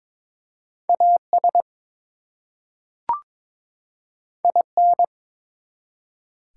Morse code, heard as a keyed tone, about 22 wpm, 700 Hz. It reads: AS IN
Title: Two Letter Words - 22wpm